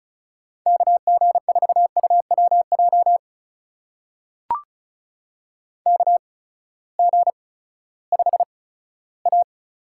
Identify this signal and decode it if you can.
Morse code, heard as a keyed tone, about 36 wpm, spaced as for 18 wpm, 700 Hz. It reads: KG4UWJ KG5A
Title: USA Call Signs - 6 Characters - 8x Character Spacing - 35wpm